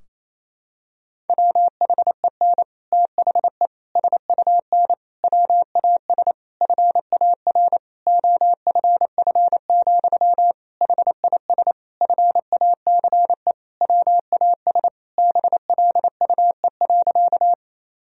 Morse code, heard as a keyed tone, about 28 wpm, 700 Hz. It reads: WHEN THE SUN WAS FAR OFF, HIS FACE WAS BLUE.
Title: Sentences from Top 1000 Words - Review Part 3 - 28wpm